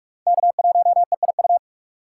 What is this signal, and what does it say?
Morse code, heard as a keyed tone, about 43 wpm, 700 Hz. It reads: K1EIU
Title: USA Call Signs - 2x Character Spacing - 45wpm